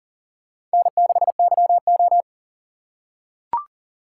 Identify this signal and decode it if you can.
Morse code, heard as a keyed tone, about 38 wpm, 700 Hz. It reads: N6YO
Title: USA Call Signs - 4 Characters - 6x Character Spacing - 40wpm